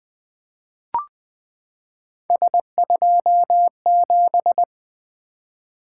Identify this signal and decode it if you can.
Morse code, heard as a keyed tone, about 20 wpm, 700 Hz. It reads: S27
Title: Three Letter-Number Combinations - 20wpm